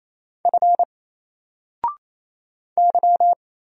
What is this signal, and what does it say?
Morse code, heard as a keyed tone, about 27 wpm, 700 Hz. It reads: F Y
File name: Single-Letter-Number-28wpm